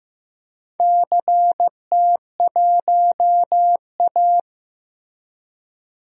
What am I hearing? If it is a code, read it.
Morse code, heard as a keyed tone, about 15 wpm, 700 Hz. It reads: CT1A
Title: DXCC-100 Call Signs - 4 Characters - 15wpm